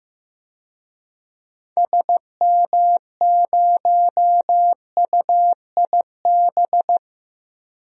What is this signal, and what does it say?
Morse code, heard as a keyed tone, about 15 wpm, 700 Hz. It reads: SM0UIB
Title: DXCC-100 Call Signs - 6 Characters - Repeated 3x - 15wpm